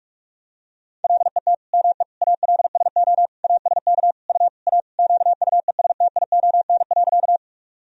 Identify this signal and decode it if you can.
Morse code, heard as a keyed tone, about 46 wpm, 700 Hz. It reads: LET ME ALSO ASK U A QUESTION.
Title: Sentences from Top 400 Words and Abbreviations 45wpm